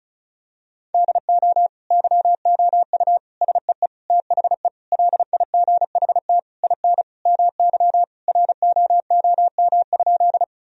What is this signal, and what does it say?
Morse code, heard as a keyed tone, about 34 wpm, 700 Hz. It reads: DO YOU SEE THE LIGHT IN MY ROOM?
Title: Sentences from Top 1000 Words - Review Part 1 - 35wpm